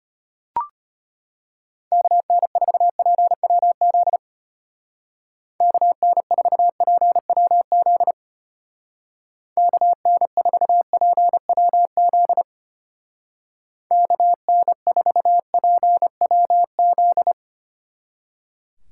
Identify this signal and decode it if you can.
Morse code, heard as a keyed tone, about 32 wpm, 700 Hz. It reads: KN4PWZ KN4PWZ KN4PWZ KN4PWZ